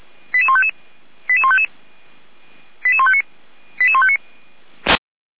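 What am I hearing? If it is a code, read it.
Five-tone selective call (ZVEI), heard as quick runs of five tones, tones of about 70 ms. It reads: 88155 88155 88159 88159
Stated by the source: ZVEI five-tone selective call received on an SDR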